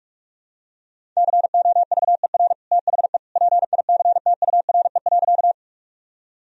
Morse code, heard as a keyed tone, about 43 wpm, 700 Hz. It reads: COVER THE PICTURE.
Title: Sentences from Top 1000 Words - Review Part 1 - 45wpm